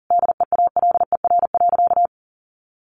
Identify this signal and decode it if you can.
Morse code, heard as a keyed tone, about 40 wpm, 700 Hz. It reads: DEALER.
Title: Sentences from Top 1100 Words 40wpm